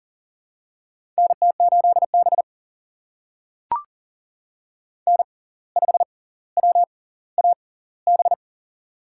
Morse code, heard as a keyed tone, about 40 wpm, spaced as for 19 wpm, 700 Hz. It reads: NT8B N5WAB
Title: USA Call Signs - 6x Character Spacing - 40wpm